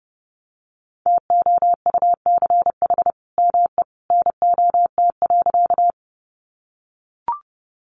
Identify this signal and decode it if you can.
Morse code, heard as a keyed tone, about 29 wpm, 700 Hz. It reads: TOUCH ME NOT.
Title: Sentences from Top 1200 Words 30wpm